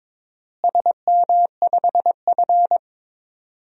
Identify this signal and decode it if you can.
Morse code, heard as a keyed tone, about 21 wpm, 700 Hz. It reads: SM5F
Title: DXCC-100 Call Signs - 4 Characters - Repeated 2x - 22wpm